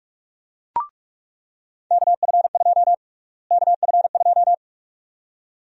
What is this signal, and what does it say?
Morse code, heard as a keyed tone, about 46 wpm, 700 Hz. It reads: KF2 KF2
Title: USA Call Sign Prefixes - Repeated 2x - 45wpm